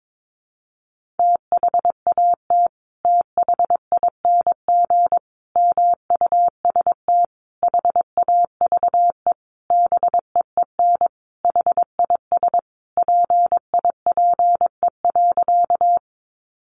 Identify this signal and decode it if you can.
Morse code, heard as a keyed tone, about 22 wpm, 700 Hz. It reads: THAT THING MUST HAVE BEEN HIS PIPE.
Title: Sentences from Top 1500 Words 22wpm